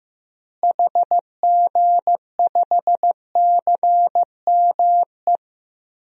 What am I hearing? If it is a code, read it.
Morse code, heard as a keyed tone, about 15 wpm, 700 Hz. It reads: HG5CME
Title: DXCC-100 Call Signs - 6 Characters - Repeated 3x - 15wpm